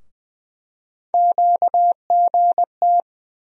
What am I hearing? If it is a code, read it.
Morse code, heard as a keyed tone, about 20 wpm, 700 Hz. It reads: QGT
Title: USA Call Sign Postfixes - 3 Letters 20wpm